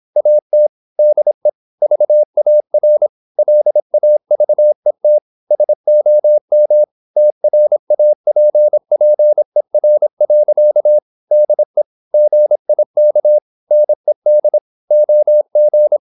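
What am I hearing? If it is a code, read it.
Morse code, heard as a keyed tone, about 26 wpm, 600 Hz. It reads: AT DE VAR LAVET SOM TRAPPER. DE GIK NED OG
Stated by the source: High-speed (QRQ) Morse at 26 wpm.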